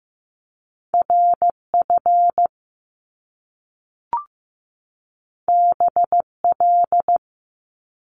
Morse code, heard as a keyed tone, about 15 wpm, 700 Hz. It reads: RF BL
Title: Top Two Letter Combinations in Words 15wpm